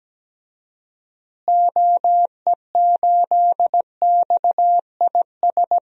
Morse code, heard as a keyed tone, about 17 wpm, 700 Hz. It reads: OE8XIS
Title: DXCC-100 Call Signs - 6 Characters - Repeated 3x - 17wpm